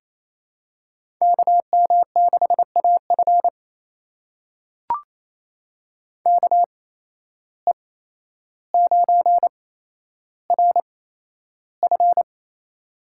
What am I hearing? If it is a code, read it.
Morse code, heard as a keyed tone, about 28 wpm, spaced as for 12 wpm, 700 Hz. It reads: KM6AF KE9RF
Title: USA Call Signs - 5 Characters - 8x Character Spacing - 28wpm